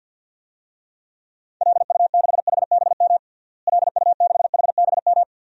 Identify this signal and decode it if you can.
Morse code, heard as a keyed tone, about 53 wpm, 700 Hz. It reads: LU6HBM LU6HBM
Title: DXCC-100 Call Signs - 6 Characters - Repeated 2x - 50wpm